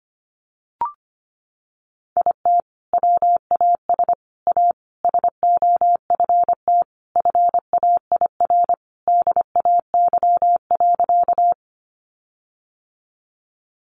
Morse code, heard as a keyed tone, about 25 wpm, 700 Hz. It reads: IT WAS A SOFT FAIR DAY.
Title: Sentences from Top 1500 Words 25wpm